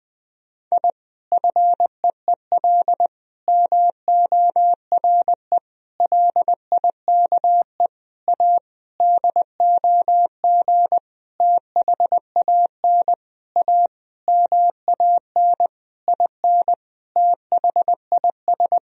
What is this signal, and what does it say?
Morse code, heard as a keyed tone, about 20 wpm, 700 Hz. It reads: I FEEL MORE LIKE A DOG THAN A MAN IN THIS
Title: Sentences from Top 1000 Words - Review Part 2 - 20wpm